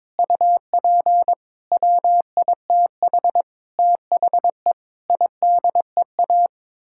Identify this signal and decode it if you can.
Morse code, heard as a keyed tone, about 22 wpm, 700 Hz. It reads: UP WITH THE IDEA
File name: practice / 22wpm / Sets of 8 Words - 22wpm